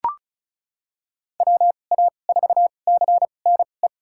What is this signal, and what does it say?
Morse code, heard as a keyed tone, about 34 wpm, spaced as for 25 wpm, 700 Hz. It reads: WA4CNE
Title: USA Call Signs - 6 Characters - 2x Character Spacing - 35wpm